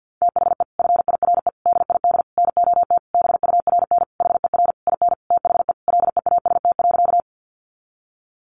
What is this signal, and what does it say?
Morse code, heard as a keyed tone, about 50 wpm, 700 Hz. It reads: THE FIRE DID NOT BURN HER IN THE LEAST.
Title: Sentences from Top 1300 Words 50wpm